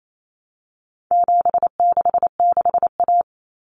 Morse code, heard as a keyed tone, about 28 wpm, 700 Hz. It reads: 766A
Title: Common Cut Numbers - 4-Digit 28wpm